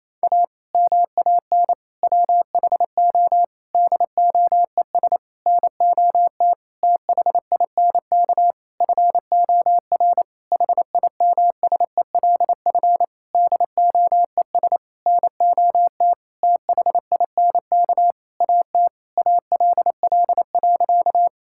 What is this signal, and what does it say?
Morse code, heard as a keyed tone, about 28 wpm, 700 Hz. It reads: A MAN WHO DOES NOT THINK FOR HIMSELF DOES NOT THINK AT ALL.
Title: Sentences from Top 1600 Words 28wpm